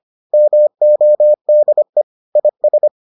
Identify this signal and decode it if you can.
Morse code, heard as a keyed tone, about 25 wpm, 600 Hz. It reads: MODE IS